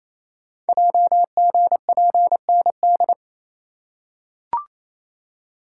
Morse code, heard as a keyed tone, about 27 wpm, 700 Hz. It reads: JGPND
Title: Code Groups - 5 Letters - 28wpm